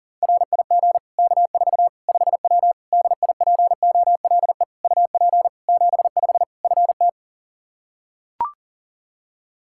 Morse code, heard as a keyed tone, about 40 wpm, 700 Hz. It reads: RIG K4 5W DIPOLE UP 75 FT
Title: Practice QSO - Exchange 2 - Info - 40wpm